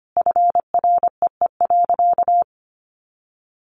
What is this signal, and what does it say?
Morse code, heard as a keyed tone, about 25 wpm, 700 Hz. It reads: FREE.
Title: Sentences from Top 1400 Words 25wpm